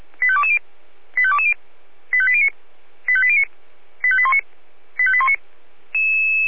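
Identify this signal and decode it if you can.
Five-tone selective call (ZVEI), heard as quick runs of five tones, tones of about 70 ms. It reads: 85220 85220 85809 85809 85819 85819
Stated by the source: ZVEI five-tone selective call received by SDR.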